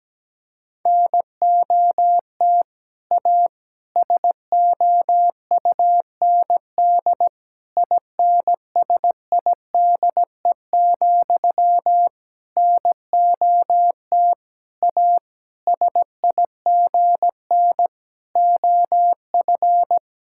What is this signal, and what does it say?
Morse code, heard as a keyed tone, about 17 wpm, 700 Hz. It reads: NOT A SOUND INSIDE, NOT A SIGN OF
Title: Sentences from Top 1000 Words - Review Part 6 - 17wpm